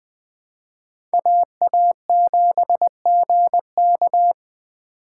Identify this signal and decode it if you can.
Morse code, heard as a keyed tone, about 24 wpm, 700 Hz. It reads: AA7GK
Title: Speed Racing - USA 2x2 Call Signs 20wpm